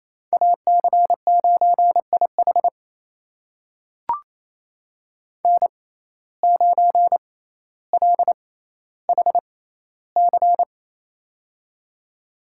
Morse code, heard as a keyed tone, about 28 wpm, spaced as for 14 wpm, 700 Hz. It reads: AC9IH N9LHC
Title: USA Call Signs - 6x Character Spacing - 28wpm